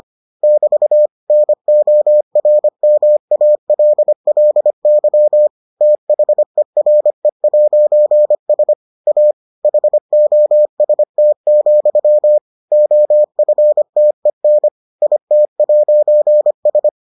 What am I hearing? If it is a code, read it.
Morse code, heard as a keyed tone, about 25 wpm, 600 Hz. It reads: <BT> NORMALLY THERE'S A HOST, OFTEN IT'S